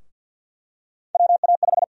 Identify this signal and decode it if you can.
Morse code, heard as a keyed tone, about 52 wpm, 700 Hz. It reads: WA5
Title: USA Call Sign Prefixes - 50wpm